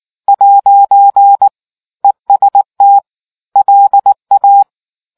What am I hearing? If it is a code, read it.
Morse code, heard as a keyed tone, about 19 wpm, 800 Hz. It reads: ' EST LA